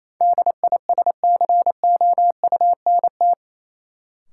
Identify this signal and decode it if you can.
Morse code, heard as a keyed tone, about 28 wpm, 700 Hz. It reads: DISCOUNT
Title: Sentences from Top 1200 Words - New Word Drill Alt - 28wpm